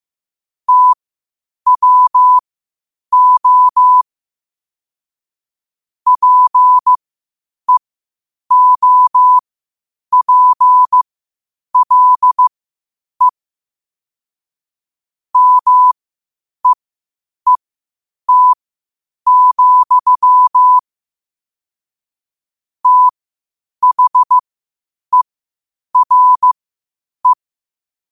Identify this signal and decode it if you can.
Morse code, heard as a keyed tone, about 15 wpm, spaced as for 8 wpm, 1000 Hz. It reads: TWO PEOPLE MEET, THERE
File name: qotdfiles / 8wpm